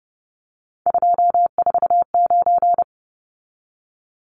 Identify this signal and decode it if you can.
Morse code, heard as a keyed tone, about 30 wpm, 700 Hz. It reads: 249
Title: Numbers - 3-Digit 30wpm